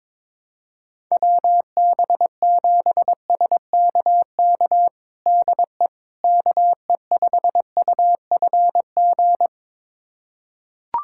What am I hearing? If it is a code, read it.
Morse code, heard as a keyed tone, about 22 wpm, 700 Hz. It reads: WB7SKK DE KE5UFG
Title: USA Callsign DE Callsign - 6 Characters - 22wpm